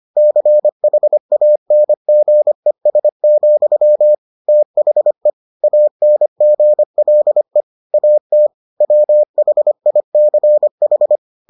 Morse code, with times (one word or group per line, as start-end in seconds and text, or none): 0.2-4.1 CHANGES,
4.5-5.3 THE
5.6-7.6 ANGLE
7.9-8.5 AT
8.8-11.2 WHICH